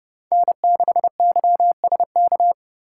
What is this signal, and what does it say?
Morse code, heard as a keyed tone, about 37 wpm, 700 Hz. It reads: N6YSK
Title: Speed Racing - USA 1x3 Call Signs 30wpm